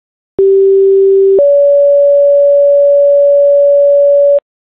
Two-tone page, heard as a one-second tone, then a three-second tone, about 390 then 570 Hz, with noise. It is heard from 0.4 to 4.4 s.